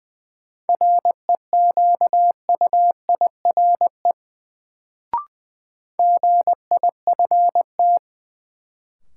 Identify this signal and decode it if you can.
Morse code, heard as a keyed tone, about 20 wpm, 700 Hz. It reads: REQUIRE GIFT
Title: Sentences from Top 900 Words - New Word Drill - 20wpm